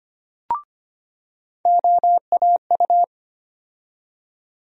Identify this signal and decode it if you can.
Morse code, heard as a keyed tone, about 25 wpm, 700 Hz. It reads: OAU
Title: USA Call Sign Postfixes - 3 Letters 25wpm